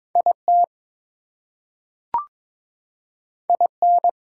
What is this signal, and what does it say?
Morse code, heard as a keyed tone, about 22 wpm, 700 Hz. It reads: IT IN
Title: Two Letter Words - 22wpm